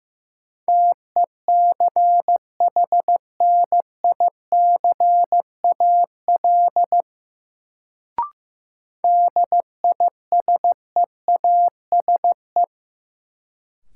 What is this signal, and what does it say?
Morse code, heard as a keyed tone, about 16 wpm, 700 Hz. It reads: TECHNICAL DISEASE
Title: Sentences from Top 1100 Words - New Word Drill Alt - 15wpm